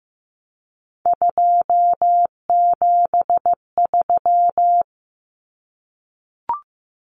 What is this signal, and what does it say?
Morse code, heard as a keyed tone, about 15 wpm, 700 Hz. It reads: 273
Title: Numbers - 3-Digit 15wpm